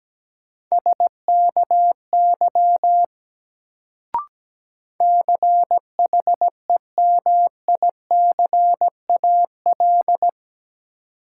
Morse code, heard as a keyed tone, about 17 wpm, 700 Hz. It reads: SKY CHEMICAL
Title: Sentences from Top 1200 Words - New Word Drill - 17wpm